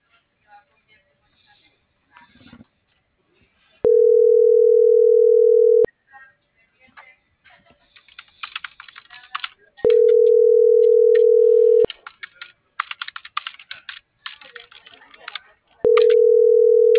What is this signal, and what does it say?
Ringback tone (North America), 440 + 480 Hz.